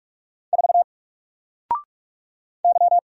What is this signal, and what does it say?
Morse code, heard as a keyed tone, about 44 wpm, 700 Hz. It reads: 4 Y
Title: Single Letter-Number 45wpm